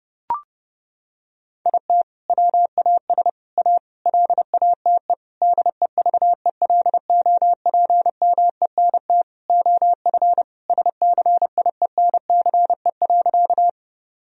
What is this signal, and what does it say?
Morse code, heard as a keyed tone, about 30 wpm, 700 Hz. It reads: IT WAS A LATE DEVELOPMENT OF SCIENCE.
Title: Sentences from Top 1000 Words - Review Part 4 - 30wpm